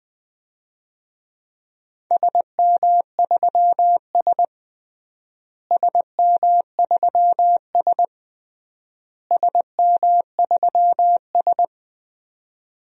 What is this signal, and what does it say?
Morse code, heard as a keyed tone, about 20 wpm, 700 Hz. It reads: SM3S SM3S SM3S
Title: DXCC-100 Call Signs - 4 Characters - Repeated 3x - 20wpm